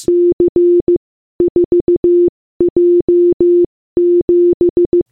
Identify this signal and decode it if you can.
Morse code, heard as a keyed tone, about 15 wpm, spaced as for 13 wpm, 350 Hz. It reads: C4J7